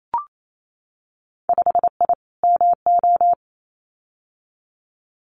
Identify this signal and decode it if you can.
Morse code, heard as a keyed tone, about 28 wpm, 700 Hz. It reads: 5I MO
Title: Winter Field Day Exchanges Encore 28wpm